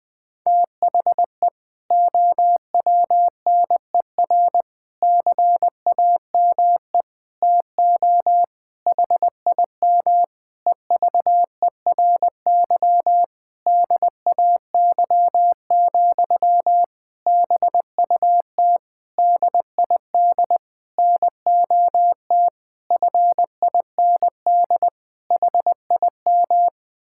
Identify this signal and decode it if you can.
Morse code, heard as a keyed tone, about 20 wpm, 700 Hz. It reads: THE OWNER CAME TO HIM EVERY DAY, BUT DID NOT FIND HIM
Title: Sentences from Top 1600 Words 20wpm